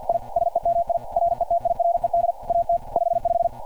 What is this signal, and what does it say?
Morse code, heard as a keyed tone, about 52 wpm, 690 Hz. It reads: I SEMI RETIRED AT 55
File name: archived / CRSnet